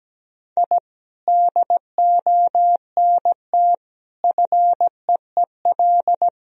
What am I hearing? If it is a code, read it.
Morse code, heard as a keyed tone, about 17 wpm, 700 Hz. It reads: I DONT FEEL
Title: Sentences from Top 1000 Words - Review Part 3 - 17wpm